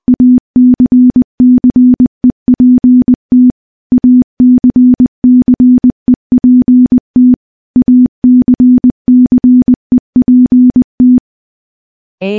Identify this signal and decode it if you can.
Morse code, heard as a keyed tone, about 20 wpm, 250 Hz. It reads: ACCEPT ACCEPT ACCEPT
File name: morse-training-20WPM